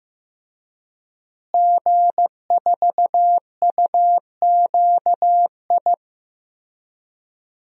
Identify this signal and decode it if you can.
Morse code, heard as a keyed tone, about 16 wpm, 700 Hz. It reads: G4UQI
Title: DXCC-100 Call Signs - 5 Characters - Repeated 2x - 15wpm